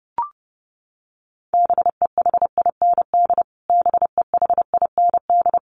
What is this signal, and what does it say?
Morse code, heard as a keyed tone, about 30 wpm, 700 Hz. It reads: BEHIND BEHIND
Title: Top 500 Words Repeated 30wpm